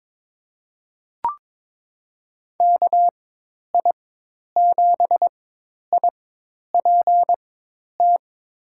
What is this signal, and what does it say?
Morse code, heard as a keyed tone, about 22 wpm, spaced as for 10 wpm, 700 Hz. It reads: KI7IPT
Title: USA Call Signs - 6 Characters - 4x Character Spacing - 22wpm